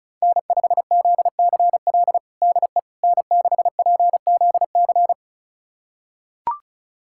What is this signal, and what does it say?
Morse code, heard as a keyed tone, about 36 wpm, 700 Hz. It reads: N5ZCL DE N6PZC
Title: USA Callsign DE Callsign 35wpm